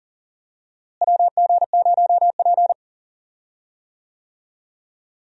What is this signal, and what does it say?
Morse code, heard as a keyed tone, about 40 wpm, 700 Hz. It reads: WG0P
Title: USA Call Signs - 4 Characters - Repeated 3x - 40wpm